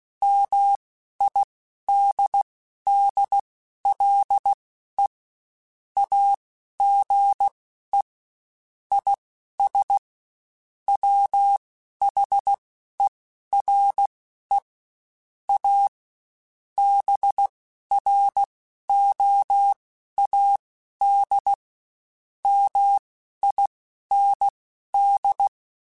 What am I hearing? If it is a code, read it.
Morse code, heard as a keyed tone, about 16 wpm, spaced as for 12 wpm, 800 Hz. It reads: MIDDLE AGE IS WHERE A BROAD MIND
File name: Spaced / Chuckles Spaced-mp3s / MiddleAge sp @15